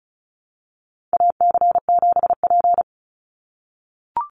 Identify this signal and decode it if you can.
Morse code, heard as a keyed tone, about 34 wpm, 700 Hz. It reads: AC7P
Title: NA SOTA Chaser Callsigns 35wpm